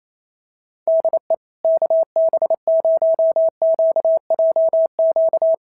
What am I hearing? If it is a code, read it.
Morse code, heard as a keyed tone, about 28 wpm, 650 Hz. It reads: DE KB0QJQ
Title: SOTA QSO Practice 28wpm